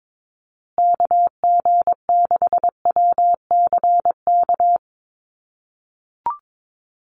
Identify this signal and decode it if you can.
Morse code, heard as a keyed tone, about 22 wpm, 700 Hz. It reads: KG6WCK
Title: USA Call Signs Repeated 3x 22wpm